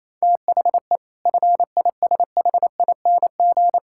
Morse code, heard as a keyed tone, about 28 wpm, 700 Hz. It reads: THE FISHING
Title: Sentences from Top 1600 Words 28wpm